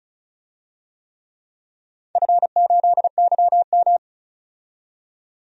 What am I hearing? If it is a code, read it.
Morse code, heard as a keyed tone, about 34 wpm, 700 Hz. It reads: F8YM
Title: DXCC-100 Call Signs - 4 Characters - 35wpm